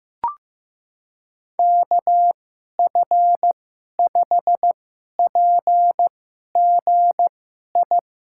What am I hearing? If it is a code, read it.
Morse code, heard as a keyed tone, about 15 wpm, spaced as for 11 wpm, 700 Hz. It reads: KF5PGI
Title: USA Call Signs - 6 Characters - 2x Character Spacing - 15wpm